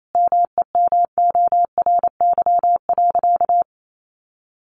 Morse code, heard as a keyed tone, about 28 wpm, 700 Hz. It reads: MEMORY.
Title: Sentences from Top 1300 Words 28wpm